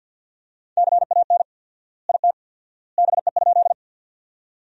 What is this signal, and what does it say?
Morse code, heard as a keyed tone, about 50 wpm, 700 Hz. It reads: CAN IT BE?
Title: 4x Word-Spacing - Sentences from Top 100 Words - 50wpm